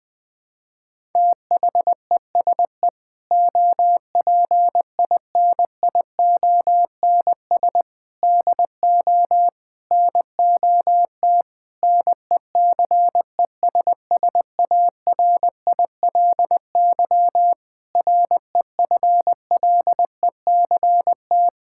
Morse code, heard as a keyed tone, about 21 wpm, 700 Hz. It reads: THESE OPINIONS DO NOT NECESSARILY REFLECT
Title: Sentences with 8-Word Phrases - Spoken Beginning - 20wpm